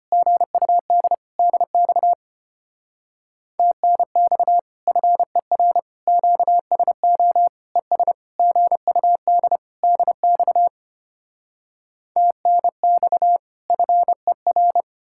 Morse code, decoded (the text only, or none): GUD DX TNX FER QSO ES GUD DX TNX FER